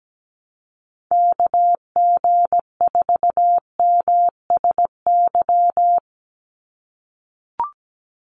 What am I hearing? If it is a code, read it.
Morse code, heard as a keyed tone, about 17 wpm, 700 Hz. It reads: KG4MSY
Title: USA Call Signs 17wpm